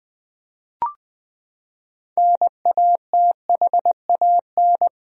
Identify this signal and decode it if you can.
Morse code, heard as a keyed tone, about 21 wpm, 700 Hz. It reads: NATHAN
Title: USA Common Names - 20wpm